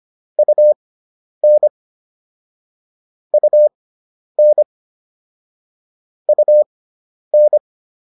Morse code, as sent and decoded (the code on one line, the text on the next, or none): ..- -. / ..- -. / ..- -.
UN UN UN